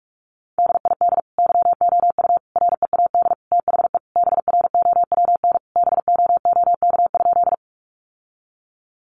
Morse code, heard as a keyed tone, about 44 wpm, 700 Hz. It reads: DID YOU READ THE BROWN BOOK?
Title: Sentences from Top 1200 Words 45wpm